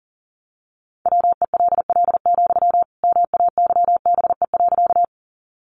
Morse code, heard as a keyed tone, about 37 wpm, 700 Hz. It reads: WELL, MAYBE.
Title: Sentences from Top 1000 Words 40wpm